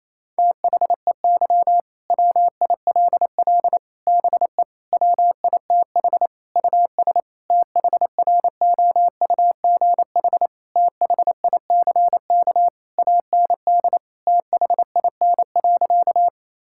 Morse code, heard as a keyed tone, about 28 wpm, 700 Hz. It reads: THEY WILL BE WITH US THROUGH THICK AND THIN.
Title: Sentences from Top 1600 Words 28wpm